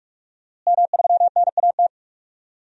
Morse code, heard as a keyed tone, about 42 wpm, 700 Hz. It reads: M3NAT
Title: DXCC-100 Call Signs - 5 Characters - Repeated 2x - 45wpm